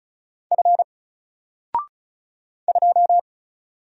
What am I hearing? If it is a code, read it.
Morse code, heard as a keyed tone, about 36 wpm, 700 Hz. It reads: F 2
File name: Single-Letter-Number-35wpm